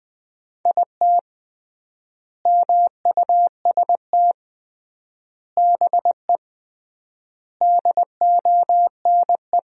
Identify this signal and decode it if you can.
Morse code, heard as a keyed tone, about 21 wpm, 700 Hz. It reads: IT MUST BE DONE
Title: 3x Word-Spacing - Sentences from Top 500 Words - 20wpm